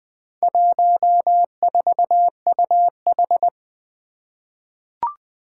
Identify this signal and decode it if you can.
Morse code, heard as a keyed tone, about 20 wpm, 700 Hz. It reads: 14UH
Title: Four Challenging Letter-Number Combinations 20wpm